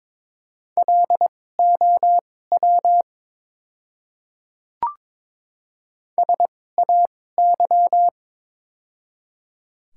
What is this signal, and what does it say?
Morse code, heard as a keyed tone, about 22 wpm, spaced as for 16 wpm, 700 Hz. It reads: LOW SAY
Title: 2x Character Spacing - Top 200 Words - 22wpm